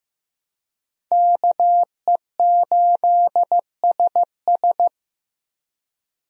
Morse code, heard as a keyed tone, about 15 wpm, 700 Hz. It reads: KE8SS
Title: USA Call Signs - 5 Characters - Repeated 2x - 15wpm